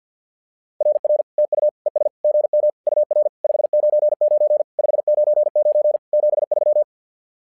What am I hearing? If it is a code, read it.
Morse code, heard as a keyed tone, about 50 wpm, 600 Hz. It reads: RR TU ES GM UR 599 599 73
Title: SOTA QSO Practice 50wpm